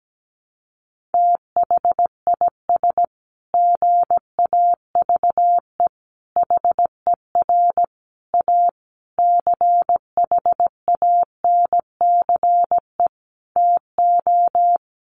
Morse code, heard as a keyed tone, about 17 wpm, 700 Hz. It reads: THIS GAVE HER A CHANCE TO
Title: Sentences from Top 1400 Words 17wpm